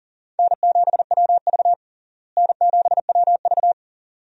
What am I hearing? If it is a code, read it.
Morse code, heard as a keyed tone, about 40 wpm, 700 Hz. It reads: N7WV N7WV
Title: USA Call Signs - 4 Characters - Repeated 2x - 40wpm